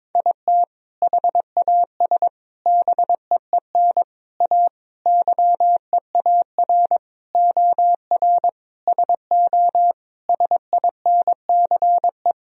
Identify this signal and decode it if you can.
Morse code, heard as a keyed tone, about 22 wpm, 700 Hz. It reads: IT HAS BEEN A YEAR OR SO SINCE
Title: Sentences from Top 1000 Words - Review Part 1 - 22wpm